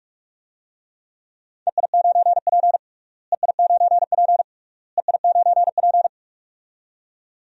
Morse code, heard as a keyed tone, about 47 wpm, 700 Hz. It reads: EI9P EI9P EI9P
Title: DXCC-100 Call Signs - 4 Characters - Repeated 3x - 45wpm